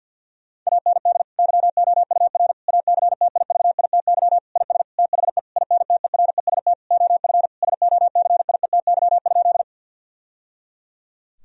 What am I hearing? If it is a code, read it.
Morse code, heard as a keyed tone, about 50 wpm, 700 Hz. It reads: AND YOUR ACTIVITY IS THE INTEREST OF SOCIETY?
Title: Sentences from Top 1000 Words - Review Part 4 - 50wpm